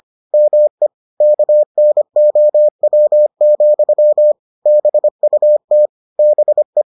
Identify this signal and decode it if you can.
Morse code, heard as a keyed tone, about 25 wpm, 600 Hz. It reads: ME KNOW, BUT BE